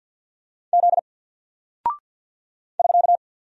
Morse code, heard as a keyed tone, about 48 wpm, 700 Hz. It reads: Z <SK>
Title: Single Character (Letter:Number:Punctuation:Prosign) Alt 50wpm